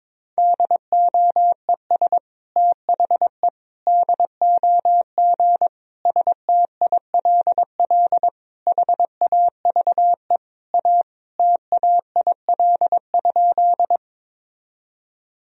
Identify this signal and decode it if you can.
Morse code, heard as a keyed tone, about 21 wpm, 700 Hz. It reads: DOES THE DOG STILL HAVE A TAIL?
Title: Sentences from Top 1000 Words - Review Part 3 - 22wpm